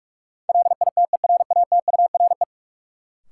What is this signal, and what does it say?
Morse code, heard as a keyed tone, about 46 wpm, 700 Hz. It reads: LITERATURE
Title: Sentences from Top 600 Words - New Word Drill - 45wpm